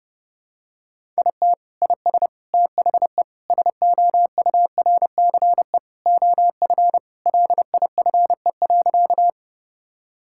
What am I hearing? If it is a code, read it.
Morse code, heard as a keyed tone, about 30 wpm, 700 Hz. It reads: IT IS THE SOURCE OF LIFE.
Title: Sentences from Top 1000 Words - Review Part 5 - 30wpm